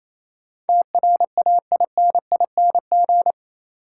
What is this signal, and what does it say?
Morse code, heard as a keyed tone, about 28 wpm, 700 Hz. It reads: TRAINING
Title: Sentences from Top 1000 Words - Review Part 4 - New Word Drill - 28wpm